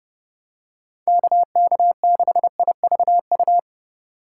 Morse code, heard as a keyed tone, about 30 wpm, 700 Hz. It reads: KK6IVU